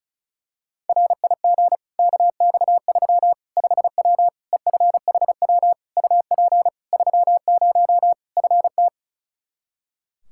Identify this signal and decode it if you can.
Morse code, heard as a keyed tone, about 36 wpm, 700 Hz. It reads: RIG KX3 5W EFHW UP 30 FT
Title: Practice QSO - Exchange 2 - Info - 35wpm